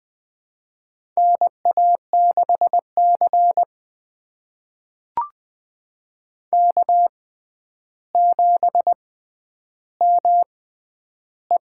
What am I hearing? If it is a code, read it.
Morse code, heard as a keyed tone, about 21 wpm, spaced as for 10 wpm, 700 Hz. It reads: NA6C K7ME
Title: USA Call Signs - 4 Characters - 6x Character Spacing - 20wpm